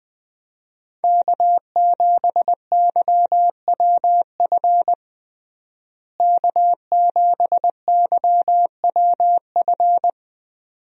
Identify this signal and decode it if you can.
Morse code, heard as a keyed tone, about 20 wpm, 700 Hz. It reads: K7YWF K7YWF
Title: USA Call Signs - 5 Characters - Repeated 2x - 20wpm